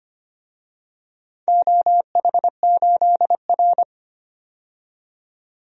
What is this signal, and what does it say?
Morse code, heard as a keyed tone, about 25 wpm, 700 Hz. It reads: OH8R
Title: DXCC-100 Call Signs - 4 Characters - 25wpm